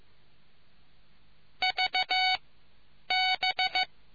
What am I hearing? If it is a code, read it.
Morse code, heard as a keyed tone, about 15 wpm, 750 Hz. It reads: V B